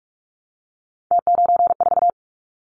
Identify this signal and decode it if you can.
Morse code, heard as a keyed tone, about 46 wpm, 690 Hz. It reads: T94
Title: Callsign Prefixes for Top 50 DXCC - 45wpm